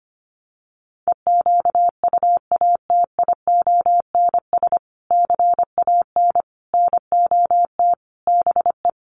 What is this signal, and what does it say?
Morse code, heard as a keyed tone, about 25 wpm, 700 Hz. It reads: EQUATIONS CAN NOT BE
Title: Sets of 5 Words 25wpm